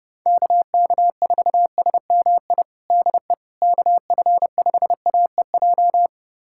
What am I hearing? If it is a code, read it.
Morse code, heard as a keyed tone, about 30 wpm, 700 Hz. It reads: KK4SMI DE KF5AEJ
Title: USA Callsign DE Callsign - 6 Characters - 30wpm